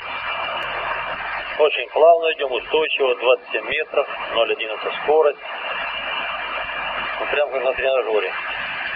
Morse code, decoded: none

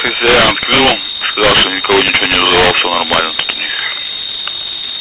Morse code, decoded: none

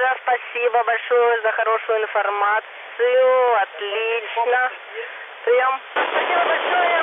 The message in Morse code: none

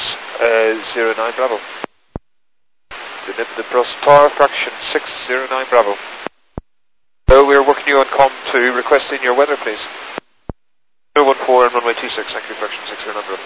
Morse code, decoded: none

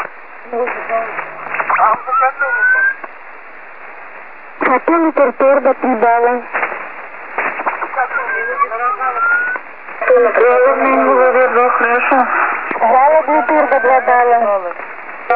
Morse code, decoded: none

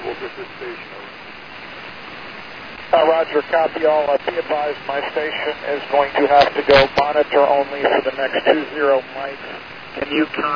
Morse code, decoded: none

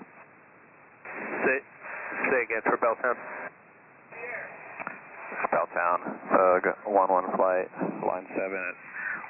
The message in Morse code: none

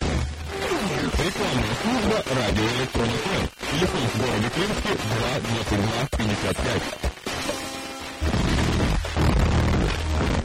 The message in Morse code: none

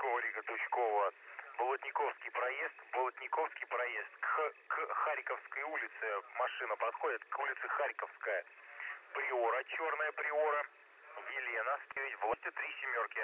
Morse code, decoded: none

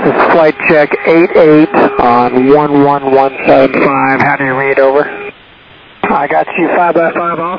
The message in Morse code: none

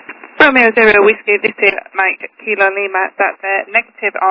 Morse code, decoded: none

